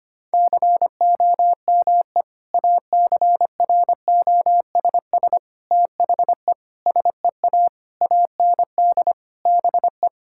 Morse code, heard as a keyed tone, about 25 wpm, 700 Hz. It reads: COME ACROSS THE SEA AND BE
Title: Sentences from Top 1000 Words - Review Part 5 - 25wpm